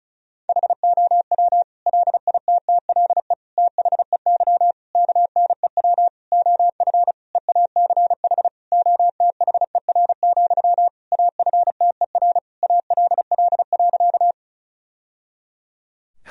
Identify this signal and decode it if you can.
Morse code, heard as a keyed tone, about 36 wpm, 700 Hz. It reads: HOW LITTLE THEY KNEW OF EACH OTHER, AFTER ALL.
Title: Sentences from Top 1000 Words - Review Part 1 - 35wpm